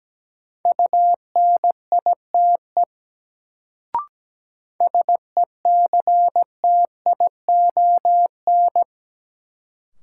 Morse code, heard as a keyed tone, about 17 wpm, 700 Hz. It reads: UNITE SECTION
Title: Sentences from Top 1000 Words - Review Part 5 - New Word Drill - 17wpm